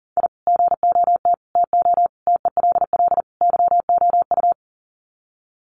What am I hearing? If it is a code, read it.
Morse code, heard as a keyed tone, about 40 wpm, 700 Hz. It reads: I GOT TO TELL YOU
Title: Sets of 5 Words Encore2 40wpm